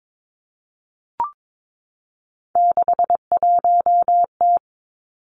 Morse code, heard as a keyed tone, about 22 wpm, 700 Hz. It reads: 61T
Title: Three Challenging Letter-Number Combinations 22wpm